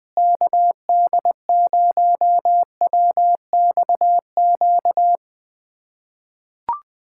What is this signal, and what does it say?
Morse code, heard as a keyed tone, about 21 wpm, 700 Hz. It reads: KD0WXQ
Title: USA Callsign DE Callsign - 6 Characters - 20wpm